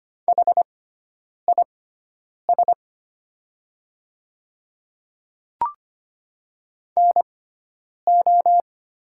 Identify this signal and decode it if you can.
Morse code, heard as a keyed tone, about 25 wpm, spaced as for 9 wpm, 700 Hz. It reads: HIS NO
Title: 6x Character Spacing - Top 100 Words - 25wpm